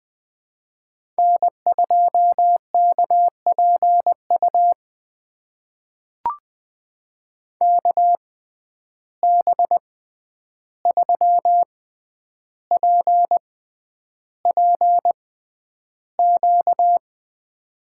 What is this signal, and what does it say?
Morse code, heard as a keyed tone, about 20 wpm, spaced as for 10 wpm, 700 Hz. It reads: N2KPU KB3PPQ
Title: USA Call Signs - 6x Character Spacing - 20wpm